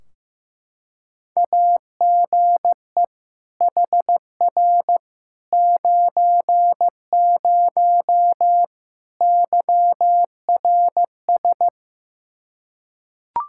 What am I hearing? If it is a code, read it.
Morse code, heard as a keyed tone, about 15 wpm, 700 Hz. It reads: AGE HR 90 YRS
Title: Practice QSO - Exchange 3 - More Info - 15wpm